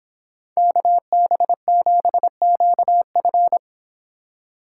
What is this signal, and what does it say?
Morse code, heard as a keyed tone, about 26 wpm, 700 Hz. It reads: KB7QF
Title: Speed Racing - USA Call Signs 22wpm